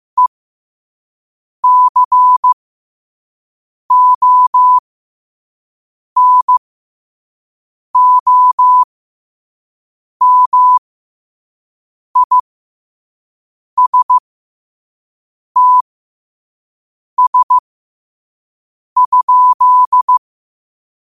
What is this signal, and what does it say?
Morse code, heard as a keyed tone, about 15 wpm, spaced as for 5 wpm, 1000 Hz. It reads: ECONOMISTS?
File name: qotdfiles / 5wpm